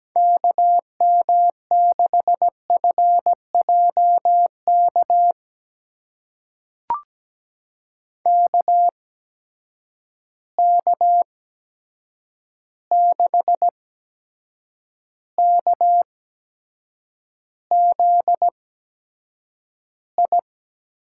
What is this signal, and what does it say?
Morse code, heard as a keyed tone, about 17 wpm, spaced as for 7 wpm, 700 Hz. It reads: KM6FJK KK6KZI
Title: USA Call Signs - 6 Characters - 8x Character Spacing - 17wpm